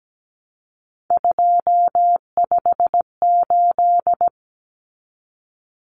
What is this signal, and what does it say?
Morse code, heard as a keyed tone, about 17 wpm, 700 Hz. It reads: 258
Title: Numbers - 3-Digit 17wpm